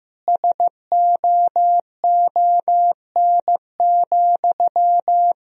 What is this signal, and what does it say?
Morse code, heard as a keyed tone, about 16 wpm, 700 Hz. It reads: SOON,
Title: Sentences from Top 1000 Words - Review Part 2 - 15wpm